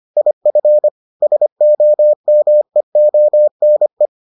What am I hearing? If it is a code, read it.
Morse code, heard as a keyed tone, about 25 wpm, 600 Hz. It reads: IF SOMEONE